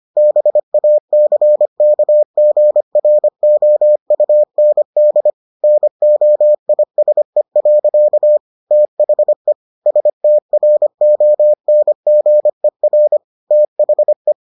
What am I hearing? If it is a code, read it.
Morse code, heard as a keyed tone, about 25 wpm, 600 Hz. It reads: BACKGROUND NOISE. THE STRONGER THE